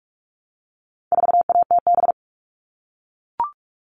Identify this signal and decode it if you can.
Morse code, heard as a keyed tone, about 44 wpm, 700 Hz. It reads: 4ATB
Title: All Cut Numbers - 4-Digit 45wpm